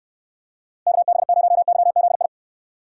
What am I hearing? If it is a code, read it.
Morse code, heard as a keyed tone, about 69 wpm, 700 Hz. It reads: KB0YZI